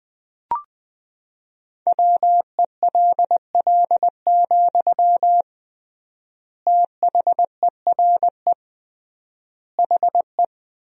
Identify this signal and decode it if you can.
Morse code, heard as a keyed tone, about 20 wpm, 700 Hz. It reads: WELL, THERE HE
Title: 3x Word-Spacing - Sentences from Top 500 Words - 20wpm